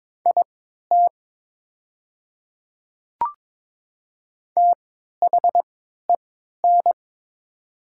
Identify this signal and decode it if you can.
Morse code, heard as a keyed tone, about 22 wpm, spaced as for 13 wpm, 700 Hz. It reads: IT THEN
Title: 3x Character Spacing - Top 100 Words - 22wpm